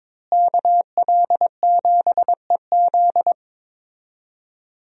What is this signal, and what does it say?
Morse code, heard as a keyed tone, about 22 wpm, 700 Hz. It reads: KL7EZ
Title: DXCC-100 Call Signs - 5 Characters - Repeated 3x - 22wpm